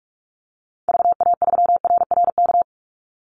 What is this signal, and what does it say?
Morse code, heard as a keyed tone, about 44 wpm, 700 Hz. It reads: VA3RRK
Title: Call Signs Repeated 3x - 45wpm